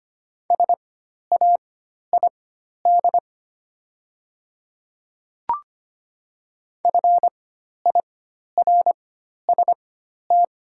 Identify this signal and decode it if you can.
Morse code, heard as a keyed tone, about 25 wpm, spaced as for 12 wpm, 700 Hz. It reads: SAID FIRST
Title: 4x Character Spacing - Top 100 Words - 25wpm